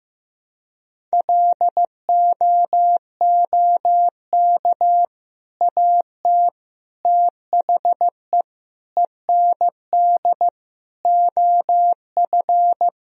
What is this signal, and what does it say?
Morse code, heard as a keyed tone, about 15 wpm, 700 Hz. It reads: LOOK AT THE END OF
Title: Sentences from Top 1000 Words - Review Part 1 - 15wpm